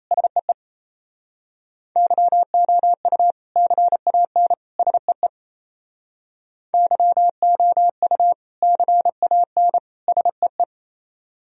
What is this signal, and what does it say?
Morse code, heard as a keyed tone, about 35 wpm, 700 Hz. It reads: SEE YOU CAN SEE YOU CAN SEE
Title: Speed-Racing - Sets of 3 Words 28wpm